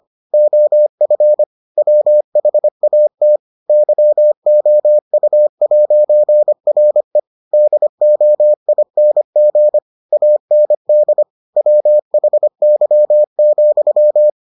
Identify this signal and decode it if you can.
Morse code, heard as a keyed tone, about 25 wpm, 600 Hz. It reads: OF WHAT YOU'RE DOING AND WHY,